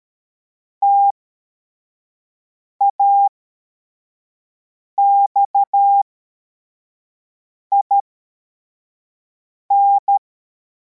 Morse code, heard as a keyed tone, about 13 wpm, spaced as for 4 wpm, 800 Hz. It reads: TAXIN
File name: formation / cw / Text43